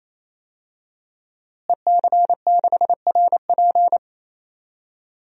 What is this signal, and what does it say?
Morse code, heard as a keyed tone, about 27 wpm, 700 Hz. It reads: EC6RP
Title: DXCC-100 Call Signs - 5 Characters - Repeated 3x - 28wpm